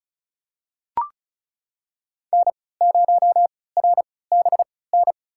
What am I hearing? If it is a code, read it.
Morse code, heard as a keyed tone, about 36 wpm, spaced as for 20 wpm, 700 Hz. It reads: N0RBN
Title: USA Call Signs - 5 Characters - 3x Character Spacing - 35wpm